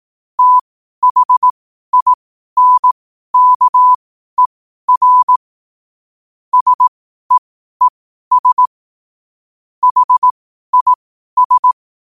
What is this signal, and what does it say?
Morse code, heard as a keyed tone, about 18 wpm, spaced as for 12 wpm, 1000 Hz. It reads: THINKER SEES HIS
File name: qotdfiles / 12wpm